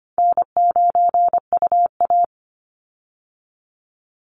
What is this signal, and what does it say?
Morse code, heard as a keyed tone, about 25 wpm, 700 Hz. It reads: N9UA
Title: Easy Call Signs - 25wpm